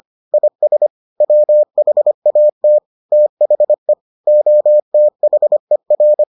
Morse code, heard as a keyed tone, about 25 wpm, 600 Hz. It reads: IS WHAT THE OTHER